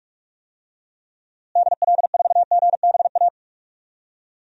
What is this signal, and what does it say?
Morse code, heard as a keyed tone, about 46 wpm, 700 Hz. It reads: DL4GBA
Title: DXCC-100 Call Signs - 6 Characters - Repeated 3x - 45wpm